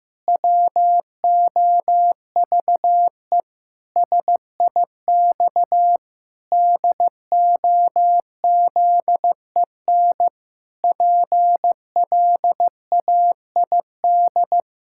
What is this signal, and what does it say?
Morse code, heard as a keyed tone, about 16 wpm, 700 Hz. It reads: WOVE SIX DOZEN PLAID
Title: Pangram Challenge v2 - 15wpm